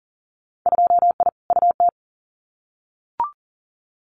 Morse code, heard as a keyed tone, about 42 wpm, 700 Hz. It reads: 2I UT
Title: Winter Field Day Exchanges 40wpm